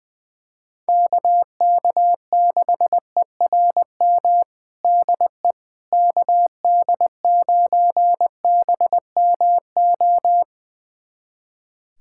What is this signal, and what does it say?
Morse code, heard as a keyed tone, about 21 wpm, 700 Hz. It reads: KK6ERM DE KD9BMO
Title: USA Callsign DE Callsign - 6 Characters - 20wpm